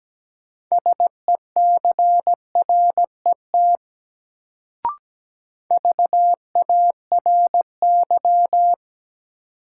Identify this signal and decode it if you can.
Morse code, heard as a keyed tone, about 17 wpm, 700 Hz. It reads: SECRET VARY
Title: Sentences from Top 1200 Words - New Word Drill Alt - 17wpm